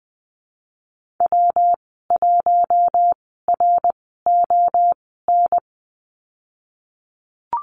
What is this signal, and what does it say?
Morse code, heard as a keyed tone, about 20 wpm, spaced as for 15 wpm, 700 Hz. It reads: W1RON
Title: 173 Lesson 28g - Number 9 - Call Signs 20wpm